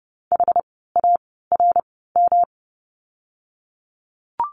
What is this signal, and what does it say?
Morse code, heard as a keyed tone, about 30 wpm, spaced as for 17 wpm, 700 Hz. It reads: HARM
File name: course / 30:10wpm / 108 Lesson 18f - Letter m - Words 30wpm